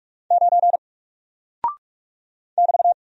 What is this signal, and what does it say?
Morse code, heard as a keyed tone, about 44 wpm, 700 Hz. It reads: 9 <BT>
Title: Single Character (Letter:Number:Punctuation:Prosign) 45wpm